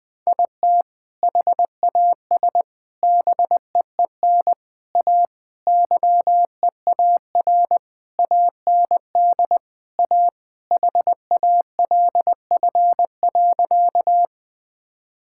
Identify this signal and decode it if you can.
Morse code, heard as a keyed tone, about 20 wpm, 700 Hz. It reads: IT HAS BEEN A YEAR AND A HALF.
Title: Sentences with 5-Word Phrases - 20wpm